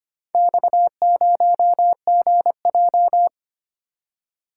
Morse code, heard as a keyed tone, about 26 wpm, 700 Hz. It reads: X0GJ
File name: practice / 25wpm / Four Letter-Number Combinations - 25wpm